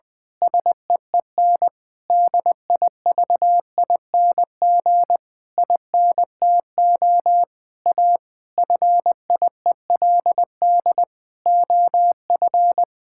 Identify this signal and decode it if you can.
Morse code, heard as a keyed tone, about 20 wpm, 700 Hz. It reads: SEEN DIVING INTO A FIELD OF
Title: Pangram Challenge v2 - 20wpm